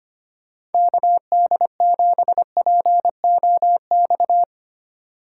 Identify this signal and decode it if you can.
Morse code, heard as a keyed tone, about 31 wpm, 700 Hz. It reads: KD7POX
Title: Speed Racing - USA 2x3 Call Signs 25wpm